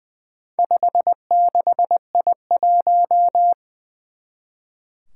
Morse code, heard as a keyed tone, about 20 wpm, 700 Hz. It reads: 56I1
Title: Four Challenging Letter-Number Combinations 20wpm